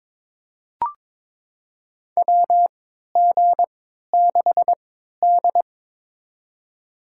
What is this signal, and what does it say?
Morse code, heard as a keyed tone, about 22 wpm, spaced as for 13 wpm, 700 Hz. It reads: WG6D
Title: USA Call Signs - 4 Characters - 3x Character Spacing - 22wpm